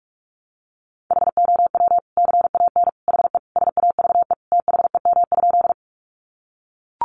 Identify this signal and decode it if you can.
Morse code, heard as a keyed tone, about 44 wpm, 700 Hz. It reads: HOW CAN HE SAVE THEM?
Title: Sentences from Top 1100 Words 45wpm